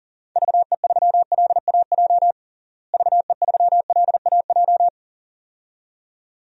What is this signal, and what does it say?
Morse code, heard as a keyed tone, about 40 wpm, 700 Hz. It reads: VE3LAJ VE3LAJ
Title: DXCC-100 Call Signs - 6 Characters - Repeated 2x - 40wpm